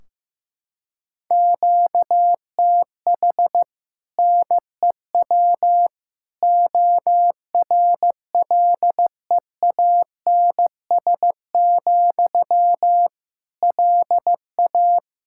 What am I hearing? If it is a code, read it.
Morse code, heard as a keyed tone, about 15 wpm, 700 Hz. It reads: QTH NEW ORLEANS, LA
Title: Practice QSO - Exchange 1 - Beginning - 15wpm